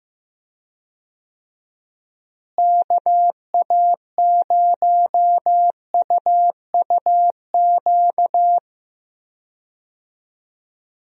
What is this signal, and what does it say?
Morse code, heard as a keyed tone, about 15 wpm, 700 Hz. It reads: KA0UUQ
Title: USA Call Signs - 6 Characters - Repeated 3x - 15wpm